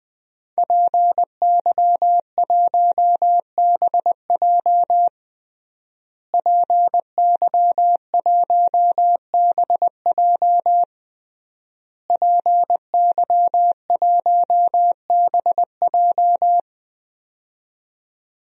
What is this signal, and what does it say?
Morse code, heard as a keyed tone, about 20 wpm, 700 Hz. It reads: PY1BJ PY1BJ PY1BJ
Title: DXCC-100 Call Signs - 5 Characters - Repeated 3x - 20wpm